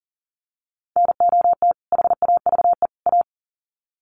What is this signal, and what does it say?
Morse code, heard as a keyed tone, about 40 wpm, 700 Hz. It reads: NOT HAVE A
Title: Sets of 3 Words 40wpm